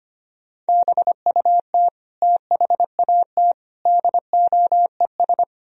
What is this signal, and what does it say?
Morse code, heard as a keyed tone, about 33 wpm, 700 Hz. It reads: BUT THAT DOES